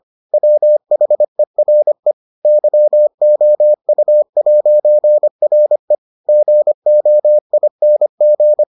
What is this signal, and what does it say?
Morse code, heard as a keyed tone, about 25 wpm, 600 Hz. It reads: WHERE YOU'RE GOING